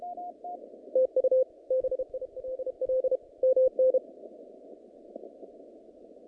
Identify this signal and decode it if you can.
Morse code, heard as a keyed tone, about 32 wpm, 520 Hz. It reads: TU BILL MN
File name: contests / naqp-cw-k5tr-op-k5pi- -2200